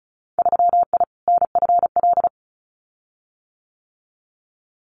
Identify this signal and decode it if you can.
Morse code, heard as a keyed tone, about 34 wpm, 700 Hz. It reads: 3I NFL
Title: Winter Field Day Exchanges 35wpm